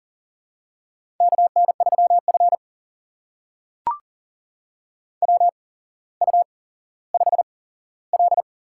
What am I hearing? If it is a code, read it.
Morse code, heard as a keyed tone, about 42 wpm, spaced as for 17 wpm, 700 Hz. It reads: KN3F WU5L
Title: USA Call Signs - 4 Characters - 8x Character Spacing - 40wpm